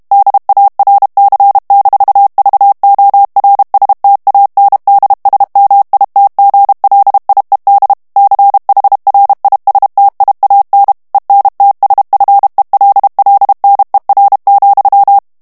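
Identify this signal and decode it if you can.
Morse code, heard as a keyed tone, about 32 wpm, 800 Hz. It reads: DARC-VORSTANDSMITGLIED CHRISTIAN ENTSFELLNER,
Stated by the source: German plain-text Morse reading practice at 30 words per minute.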